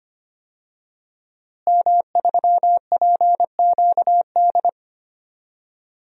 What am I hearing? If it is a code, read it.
Morse code, heard as a keyed tone, about 25 wpm, 700 Hz. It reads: M3PQD
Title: DXCC-100 Call Signs - 5 Characters - Repeated 3x - 25wpm